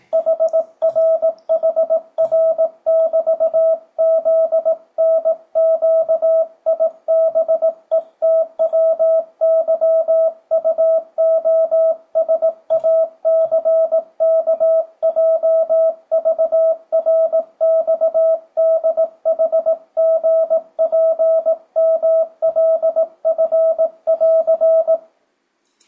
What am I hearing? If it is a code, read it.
Morse code, heard as a keyed tone, about 18 wpm, 650 Hz, with radio noise. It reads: HRHR<BT>ZNQIBETWYUOSACKJVRXDHGPMLF<AR>
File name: 2025cw / speedup / 18wpm